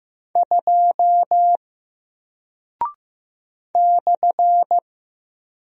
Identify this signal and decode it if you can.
Morse code, heard as a keyed tone, about 15 wpm, 700 Hz. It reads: 2 /
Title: Single Character (Letter:Number:Punctuation:Prosign) Alt 15wpm